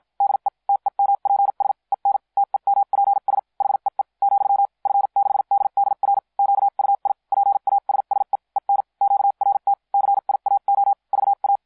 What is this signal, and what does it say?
Morse code, heard as a keyed tone, about 50 wpm, spaced as for 43 wpm, 800 Hz. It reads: DE TEMPS EN TEMPS HEE , F6DDR QUI PASSE EN QRT CIAO VA